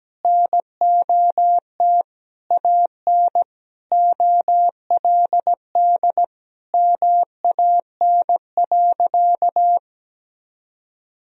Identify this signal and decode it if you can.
Morse code, heard as a keyed tone, about 17 wpm, 700 Hz. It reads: NOT AN OLD MAN.
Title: Sentences from Top 1000 Words - Review Part 4 - 17wpm